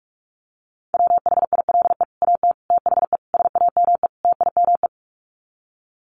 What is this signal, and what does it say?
Morse code, heard as a keyed tone, about 44 wpm, 700 Hz. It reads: WHILE AT THE SAME TIME
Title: Sets of 5 Words 45wpm